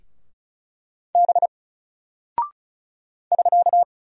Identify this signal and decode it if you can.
Morse code, heard as a keyed tone, about 36 wpm, 700 Hz. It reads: B <SK>
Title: Single Character (Letter:Number:Punctuation:Prosign) Alt 35wpm